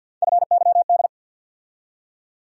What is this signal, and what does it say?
Morse code, heard as a keyed tone, about 48 wpm, 700 Hz. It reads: FYD
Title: USA Call Sign Postfixes - 3 Letters 50wpm